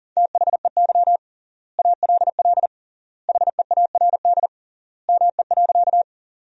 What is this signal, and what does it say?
Morse code, heard as a keyed tone, about 40 wpm, 700 Hz. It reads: THEY ALL HEARD ME.
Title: 3x Word-Spacing - Sentences from Top 400 Words - 40wpm